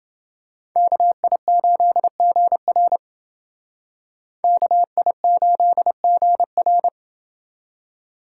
Morse code, heard as a keyed tone, about 30 wpm, 700 Hz. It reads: KI8GR KI8GR